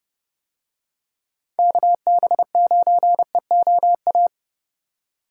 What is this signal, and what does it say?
Morse code, heard as a keyed tone, about 30 wpm, 700 Hz. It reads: KB9EOA